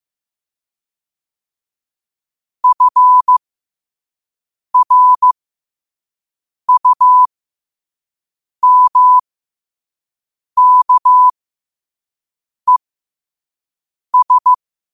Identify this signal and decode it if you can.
Morse code, heard as a keyed tone, about 15 wpm, spaced as for 5 wpm, 1000 Hz. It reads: FRUMKES
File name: qotdfiles / 5wpm